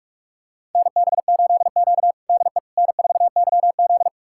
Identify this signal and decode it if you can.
Morse code, heard as a keyed tone, about 46 wpm, 700 Hz. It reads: NB8Q DE N4YZ
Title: USA Callsign DE Callsign - 4 Characters - 45wpm